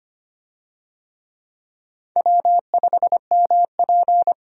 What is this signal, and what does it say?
Morse code, heard as a keyed tone, about 25 wpm, 700 Hz. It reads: W5MP
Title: USA Call Signs - 4 Characters - 25wpm